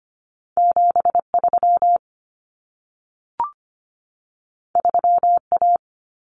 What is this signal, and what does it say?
Morse code, heard as a keyed tone, about 25 wpm, 700 Hz. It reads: 73 3A
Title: Common Cut Numbers - 2-Digit 25wpm